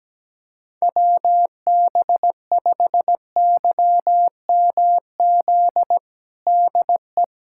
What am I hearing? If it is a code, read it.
Morse code, heard as a keyed tone, about 17 wpm, 700 Hz. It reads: WB5YMZ DE
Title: USA Callsign DE Callsign 17wpm